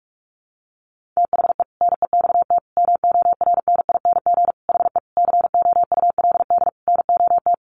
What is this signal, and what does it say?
Morse code, heard as a keyed tone, about 44 wpm, 700 Hz. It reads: THE NEXT MORNING HE COULD NOT
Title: Sentences from Top 600 Words 45wpm